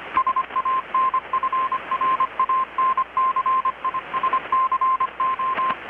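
Morse code, heard as a keyed tone, about 25 wpm, 1050 Hz, with radio noise. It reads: SANFRANCISCO
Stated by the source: Morse from KPH's Night of Nights XX maritime radiotelegraphy event, received in Chiapas, Mexico about 3,800 km away on a Kenwood R-600 with a grounded 10 m long-wire antenna.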